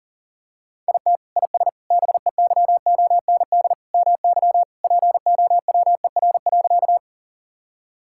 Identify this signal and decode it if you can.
Morse code, heard as a keyed tone, about 40 wpm, 700 Hz. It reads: IT IS BEYOND MY POWER.
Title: Sentences from Top 1000 Words - Review Part 6 - 40wpm